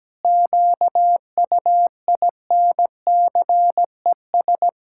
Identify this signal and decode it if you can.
Morse code, heard as a keyed tone, about 17 wpm, 700 Hz. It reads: QUINCES
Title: Pangram Challenge v2 - 17wpm